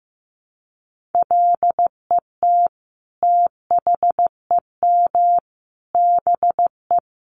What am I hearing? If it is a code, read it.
Morse code, heard as a keyed tone, about 15 wpm, 700 Hz. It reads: LET THEM BE
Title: Sentences from Top 1300 Words 15wpm